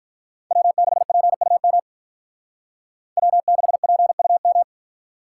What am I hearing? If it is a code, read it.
Morse code, heard as a keyed tone, about 45 wpm, 700 Hz. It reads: W6PUM W6PUM